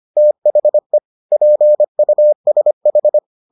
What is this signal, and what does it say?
Morse code, heard as a keyed tone, about 25 wpm, 600 Hz. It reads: THE PUSH